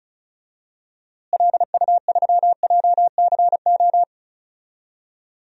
Morse code, heard as a keyed tone, about 36 wpm, 700 Hz. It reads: LU3JCO
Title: DXCC-100 Call Signs - 6 Characters - Repeated 3x - 35wpm